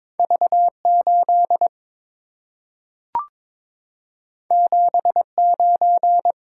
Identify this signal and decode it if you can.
Morse code, heard as a keyed tone, about 22 wpm, 690 Hz. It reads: V8 79
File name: practice / 22wpm / Two Letter-Number Combinations - 22wpm